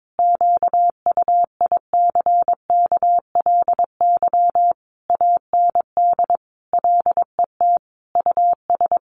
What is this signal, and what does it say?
Morse code, heard as a keyed tone, about 22 wpm, 700 Hz. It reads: QUICKLY AND LET US
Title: Sentences from Top 1300 Words 22wpm